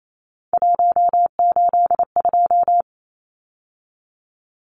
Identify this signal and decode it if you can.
Morse code, heard as a keyed tone, about 28 wpm, 700 Hz. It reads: 182
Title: Numbers - 3-Digit 28wpm